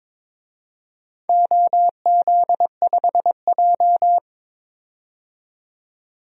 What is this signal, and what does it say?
Morse code, heard as a keyed tone, about 22 wpm, 700 Hz. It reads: OZ5J
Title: DXCC-100 Call Signs - 4 Characters - 22wpm